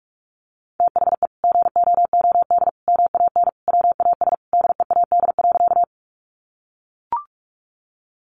Morse code, heard as a keyed tone, about 44 wpm, 700 Hz. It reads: THE GOOD MAN WAS DEAD.
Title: Sentences from Top 1500 Words 45wpm